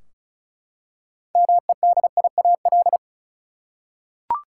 Morse code, heard as a keyed tone, about 34 wpm, 700 Hz. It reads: MEDIAL
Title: Practice QSO - Exchange 3 - More Info - 35wpm